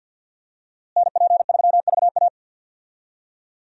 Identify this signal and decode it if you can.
Morse code, heard as a keyed tone, about 52 wpm, 700 Hz. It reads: NP3VA
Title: USA Call Signs - 5 Characters - Repeated 2x - 50wpm